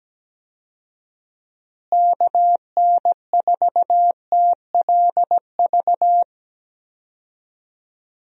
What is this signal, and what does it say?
Morse code, heard as a keyed tone, about 17 wpm, 700 Hz. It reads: KN4TLV
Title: USA Call Signs - 6 Characters - Repeated 2x - 17wpm